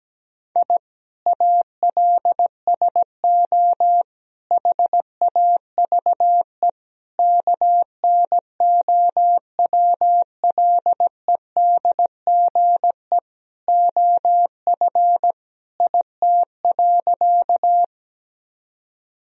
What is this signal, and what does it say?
Morse code, heard as a keyed tone, about 17 wpm, 700 Hz. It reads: I ALSO HAVE KNOWLEDGE OF IT.
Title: Sentences from Top 1000 Words - Review Part 3 - 17wpm